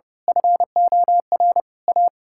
Morse code, heard as a keyed tone, about 30 wpm, 700 Hz. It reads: FOR A
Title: Pangram Challenge v2 - 30wpm